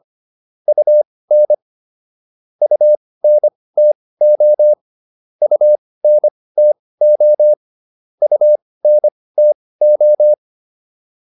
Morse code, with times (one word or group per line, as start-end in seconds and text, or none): 0.7-1.5 UN
2.6-4.7 UNTO
5.4-7.5 UNTO
8.2-10.3 UNTO